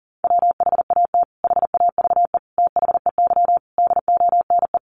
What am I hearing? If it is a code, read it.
Morse code, heard as a keyed tone, about 40 wpm, 700 Hz. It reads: WHAT HAVE THEY DONE
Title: Sentences from Top 500 Words 40wpm